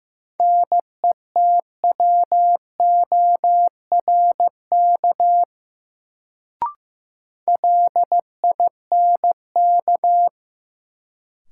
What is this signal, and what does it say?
Morse code, heard as a keyed tone, about 15 wpm, 700 Hz. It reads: NETWORK LINK
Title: Sentences from Top 1000 Words - Review Part 7 - New Word Drill - 15wpm